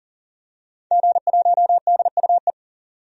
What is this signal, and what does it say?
Morse code, heard as a keyed tone, about 40 wpm, 700 Hz. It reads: G1DUE
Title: DXCC-100 Call Signs - 5 Characters - 40wpm